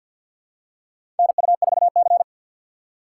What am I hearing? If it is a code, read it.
Morse code, heard as a keyed tone, about 53 wpm, 700 Hz. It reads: NU4C
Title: USA Call Signs - 4 Characters - 2x Character Spacing - 50wpm